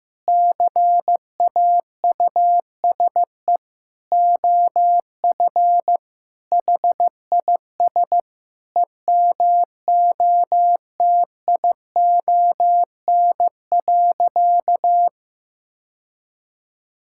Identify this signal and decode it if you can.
Morse code, heard as a keyed tone, about 15 wpm, 700 Hz. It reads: CAUSE OF HIS EMOTION.
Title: Sentences from Top 1600 Words 15wpm